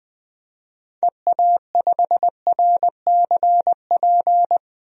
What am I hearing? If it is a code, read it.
Morse code, heard as a keyed tone, about 20 wpm, 700 Hz. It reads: EA5RCP
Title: DXCC-100 Call Signs - 6 Characters - Repeated 3x - 20wpm